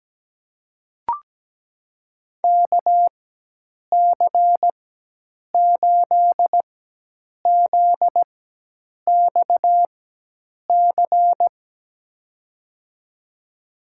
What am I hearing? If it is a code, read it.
Morse code, heard as a keyed tone, about 17 wpm, spaced as for 8 wpm, 700 Hz. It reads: KC8ZXC
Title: USA Call Signs - 4x Character Spacing - 17wpm